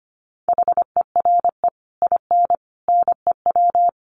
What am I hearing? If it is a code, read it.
Morse code, heard as a keyed tone, about 25 wpm, 700 Hz. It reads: HERE IN NEW
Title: Sets of 5 Words Encore2 25wpm